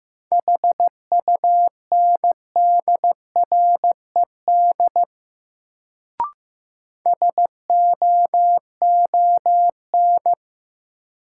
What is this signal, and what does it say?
Morse code, heard as a keyed tone, about 15 wpm, 700 Hz. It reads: HUNDRED SOON
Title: Sentences from Top 400 Words - New Word Drill - 15wpm